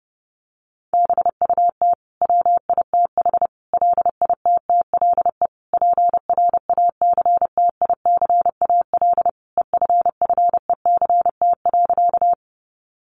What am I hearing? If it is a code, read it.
Morse code, heard as a keyed tone, about 30 wpm, 700 Hz. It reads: BUT WITH LITTLE PRACTICAL EFFECT.
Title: Sentences from Top 1400 Words 30wpm